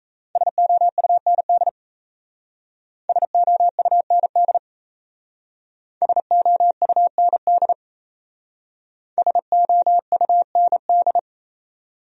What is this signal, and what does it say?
Morse code, heard as a keyed tone, about 35 wpm, 700 Hz. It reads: SOUND SOUND SOUND SOUND
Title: Speed-Racing Top 500 Words 28wpm